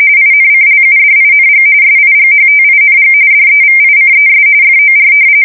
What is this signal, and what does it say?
RTTY, about 45.45 baud. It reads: INCLUDING SIGHTING DEVICES FOR
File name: fsk170 / battleshipnj